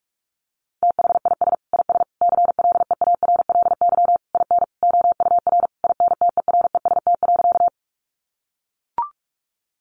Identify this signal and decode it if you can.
Morse code, heard as a keyed tone, about 44 wpm, 700 Hz. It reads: THIS IS CLEARLY IN OUR INTEREST.
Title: Sentences from Top 1300 Words 45wpm